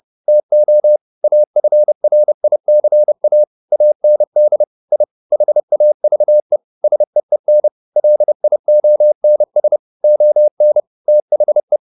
Morse code, heard as a keyed tone, about 30 wpm, 600 Hz. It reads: TO AFRICA AND I HAVE SEEN LIONS ON THE
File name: the old man and the sea / the old man and the sea 1 30wpm